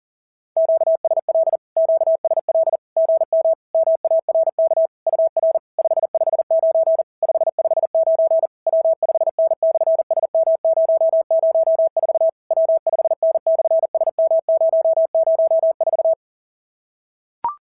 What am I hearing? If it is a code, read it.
Morse code, heard as a keyed tone, about 37 wpm, 660 Hz. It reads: QSL QSL GM MARK UR 559 559 W5N/SM004 W5N/SM004
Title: SOTA QSO Practice 40wpm